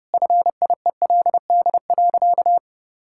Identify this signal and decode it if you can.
Morse code, heard as a keyed tone, about 30 wpm, 700 Hz. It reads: FIELD.
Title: Sentences from Top 1000 Words - Review Part 6 - 30wpm